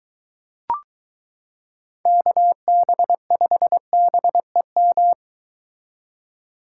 Morse code, heard as a keyed tone, about 23 wpm, 700 Hz. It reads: KB5BEM